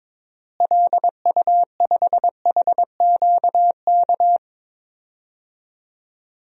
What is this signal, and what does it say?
Morse code, heard as a keyed tone, about 22 wpm, 700 Hz. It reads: LU5HQK
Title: DXCC-100 Call Signs - 6 Characters - Repeated 2x - 22wpm